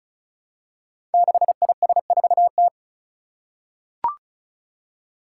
Morse code, heard as a keyed tone, about 36 wpm, 700 Hz. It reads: 6IS4T
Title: Five Challenging Letter-Number Combinations 35wpm